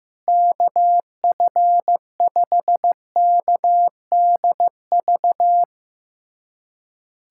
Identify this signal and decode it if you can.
Morse code, heard as a keyed tone, about 16 wpm, 700 Hz. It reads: KF5KDV
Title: USA Callsign DE Callsign 15wpm